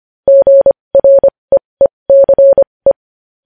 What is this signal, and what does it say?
Morse code, heard as a keyed tone, about 25 wpm, spaced as for 20 wpm, 550 Hz. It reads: GREECE
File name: podcast / News headlines-1x-25x20